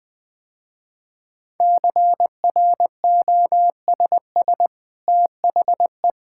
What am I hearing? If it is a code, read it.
Morse code, heard as a keyed tone, about 20 wpm, 700 Hz. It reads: CROSS THE
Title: Easy Sentences with Hints - 20wpm